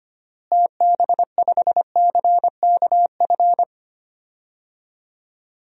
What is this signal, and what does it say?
Morse code, heard as a keyed tone, about 25 wpm, 700 Hz. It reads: TB5CKF
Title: DXCC-100 Call Signs - 6 Characters - Repeated 2x - 25wpm